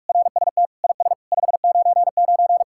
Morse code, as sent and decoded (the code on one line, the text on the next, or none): .-. ... - / .. ... / ..... ----. ----.
RST IS 599